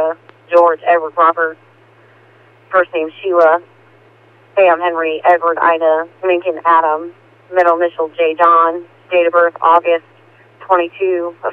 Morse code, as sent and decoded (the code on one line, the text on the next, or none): none